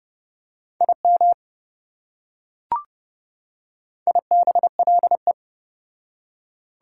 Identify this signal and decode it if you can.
Morse code, heard as a keyed tone, about 28 wpm, 700 Hz. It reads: IM IBLE